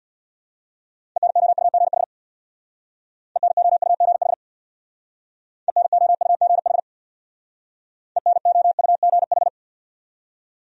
Morse code, heard as a keyed tone, about 61 wpm, 700 Hz. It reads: ENOUGH ENOUGH ENOUGH ENOUGH
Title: Speed-Racing Top 500 Words 50wpm